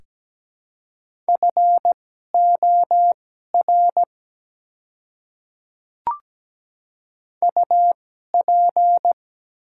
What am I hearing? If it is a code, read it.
Morse code, heard as a keyed tone, about 17 wpm, spaced as for 12 wpm, 700 Hz. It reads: FOR UP
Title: 2x Character Spacing - Top 200 Words - 17wpm